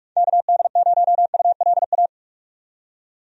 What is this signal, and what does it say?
Morse code, heard as a keyed tone, about 46 wpm, 700 Hz. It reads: KD0ULA
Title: USA Call Signs - 6 Characters - 45wpm